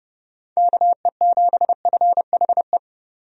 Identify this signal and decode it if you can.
Morse code, heard as a keyed tone, about 30 wpm, 700 Hz. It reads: KE7FHE
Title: USA Call Signs - 6 Characters - 30wpm